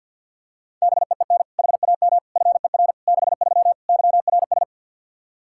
Morse code, heard as a keyed tone, about 52 wpm, 700 Hz. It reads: BEEN HAM FER 63 YRS
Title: Practice QSO - Exchange 3 - More Info - 50wpm